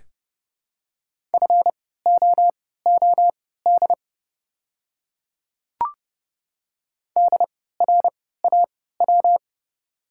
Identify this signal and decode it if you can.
Morse code, heard as a keyed tone, about 31 wpm, spaced as for 17 wpm, 690 Hz. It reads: FOOD DRAW
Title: 3x Character Spacing - Top 300 Words - 30wpm